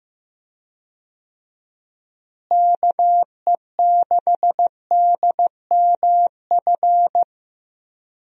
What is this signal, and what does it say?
Morse code, heard as a keyed tone, about 15 wpm, 700 Hz. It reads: KE6DMF
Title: USA Call Signs - 6 Characters - Repeated 3x - 15wpm